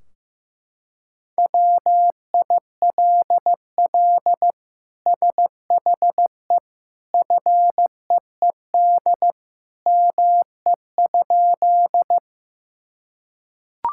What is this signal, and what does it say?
Morse code, heard as a keyed tone, about 15 wpm, 700 Hz. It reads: WILL SHE FEED ME?
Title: Sentences from Top 1600 Words 15wpm